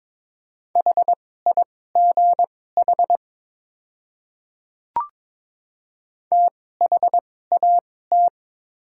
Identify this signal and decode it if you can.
Morse code, heard as a keyed tone, about 22 wpm, spaced as for 16 wpm, 690 Hz. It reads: HIGH THAT
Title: 2x Character Spacing - Top 300 Words - 22wpm